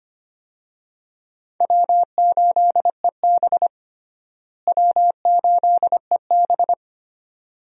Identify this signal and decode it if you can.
Morse code, heard as a keyed tone, about 25 wpm, 700 Hz. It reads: W8EB W8EB
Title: USA Call Signs - 4 Characters - Repeated 2x - 25wpm